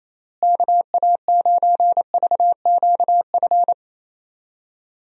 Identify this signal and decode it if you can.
Morse code, heard as a keyed tone, about 28 wpm, 700 Hz. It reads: KA9VQF
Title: USA Call Signs - 6 Characters - Repeated 2x - 28wpm